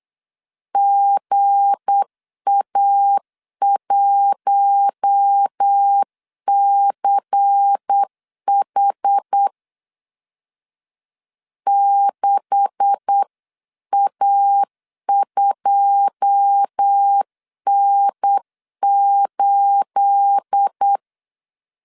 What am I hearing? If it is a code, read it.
Morse code, heard as a keyed tone, about 8 wpm, 800 Hz. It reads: GA1CH 6A2N8
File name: CW / DET / 1 9 1500C S A 8wpm 5637